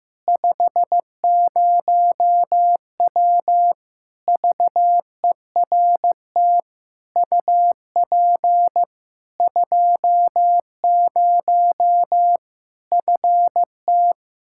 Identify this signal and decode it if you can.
Morse code, heard as a keyed tone, about 15 wpm, 700 Hz. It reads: 50W VERT UP 20 FT
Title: Practice QSO - Exchange 2 - Info - 15wpm